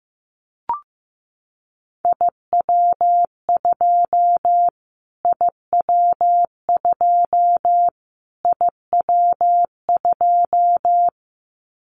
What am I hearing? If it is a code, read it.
Morse code, heard as a keyed tone, about 15 wpm, 700 Hz. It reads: IW2 IW2 IW2
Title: Callsign Prefixes for Top 25 DXCC - Repeated 3x - 15wpm